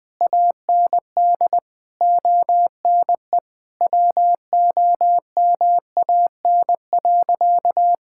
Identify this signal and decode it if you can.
Morse code, heard as a keyed tone, about 21 wpm, 700 Hz. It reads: AND ONE WOMAN.
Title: Sentences with 6-Word Phrases - 20wpm